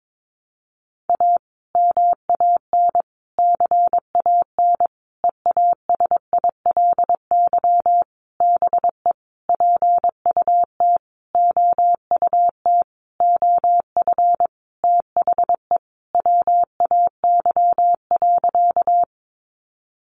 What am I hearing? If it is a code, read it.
Morse code, heard as a keyed tone, about 22 wpm, 700 Hz. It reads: A MAN CAN EASILY BE PUT OUT OF THE WAY.
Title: Sentences from Top 900 Words 22wpm